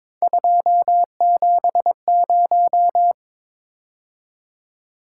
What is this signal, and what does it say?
Morse code, heard as a keyed tone, about 26 wpm, 700 Hz. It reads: 270
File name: Speed-Racing - Three Letter-Number Combinations - 22wpm